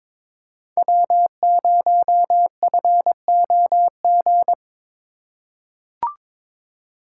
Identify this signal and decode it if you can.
Morse code, heard as a keyed tone, about 22 wpm, 700 Hz. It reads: W0FOG
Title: USA Call Signs - 5 Characters - 3x Character Spacing - 22wpm